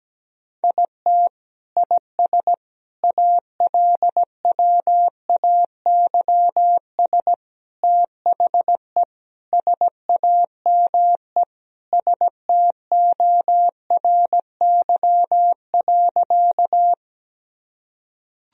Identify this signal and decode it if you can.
Morse code, heard as a keyed tone, about 17 wpm, 700 Hz. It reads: IT IS ALWAYS THE SAME STORY.
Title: Sentences from Top 1000 Words - Review Part 1 - 17wpm